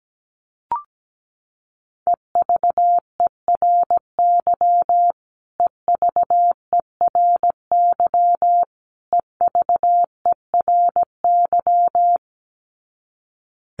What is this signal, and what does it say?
Morse code, heard as a keyed tone, about 17 wpm, 700 Hz. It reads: EVERY EVERY EVERY
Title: Top 500 Words Repeated 17wpm